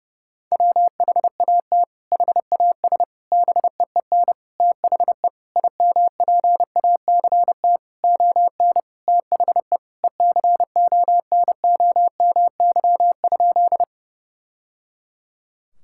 Morse code, heard as a keyed tone, about 30 wpm, 700 Hz. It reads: WHAT HAS BEEN THE IMPACT ON THE ECONOMY?
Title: Sentences from Top 1000 Words - Review Part 6 - 30wpm